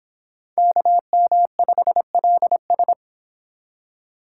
Morse code, heard as a keyed tone, about 26 wpm, 700 Hz. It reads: KM5LS